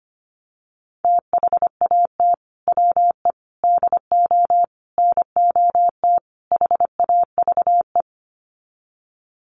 Morse code, heard as a keyed tone, about 25 wpm, 700 Hz. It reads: THAT WE DO NOT HAVE
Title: Sets of 5 Words Encore 25wpm